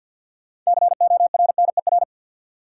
Morse code, heard as a keyed tone, about 53 wpm, 700 Hz. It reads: CORNER
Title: Sentences from Top 1600 Words - New Word Drill Alt - 50wpm